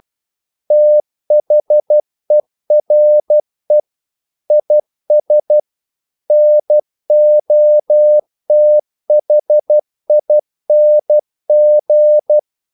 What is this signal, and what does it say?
Morse code, heard as a keyed tone, about 12 wpm, 600 Hz. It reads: THERE IS NOTHING
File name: CW / Practice / If12